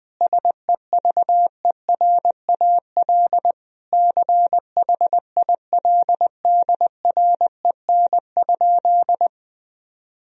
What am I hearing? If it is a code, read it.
Morse code, heard as a keyed tone, about 20 wpm, 700 Hz. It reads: SEVERAL CHILDREN?
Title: Sentences from Top 1000 Words - Review Part 2 - 20wpm